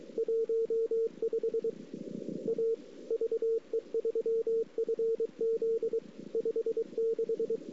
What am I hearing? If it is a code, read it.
Morse code, heard as a keyed tone, about 27 wpm, 410 Hz. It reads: 15 A VE3FZ 56
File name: CW / 2008-ss-cw-k5tr-op-n5rz -01:00